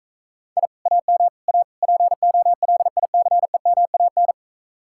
Morse code, heard as a keyed tone, about 44 wpm, 700 Hz. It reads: I AM A POLICEMAN